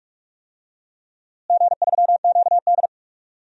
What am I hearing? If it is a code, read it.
Morse code, heard as a keyed tone, about 41 wpm, 700 Hz. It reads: G3QD
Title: DXCC-100 Call Signs - 4 Characters - Repeated 3x - 45wpm